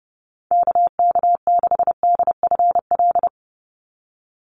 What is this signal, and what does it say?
Morse code, heard as a keyed tone, about 30 wpm, 700 Hz. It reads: KK6DFL
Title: USA Call Signs 30wpm